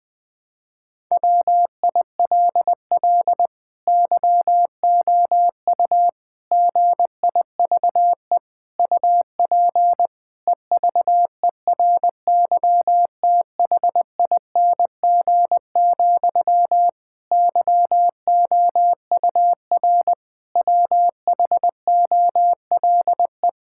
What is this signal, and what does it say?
Morse code, heard as a keyed tone, about 21 wpm, 700 Hz. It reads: WILL YOU GIVE UP EVERYTHING, YOUR WHOLE
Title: Sentences from Top 1000 Words - Review Part 5 - 20wpm